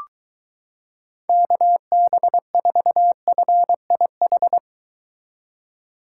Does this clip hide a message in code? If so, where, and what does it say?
Yes: Morse code from 1.3 to 4.6 s, heard as a keyed tone, about 23 wpm, 700 Hz. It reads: KB4FIH